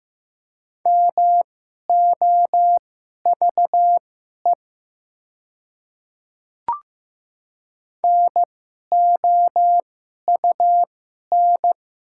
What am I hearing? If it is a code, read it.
Morse code, heard as a keyed tone, about 15 wpm, spaced as for 11 wpm, 700 Hz. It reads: MOVE NOUN
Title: 2x Character Spacing - Top 500 Words - 15wpm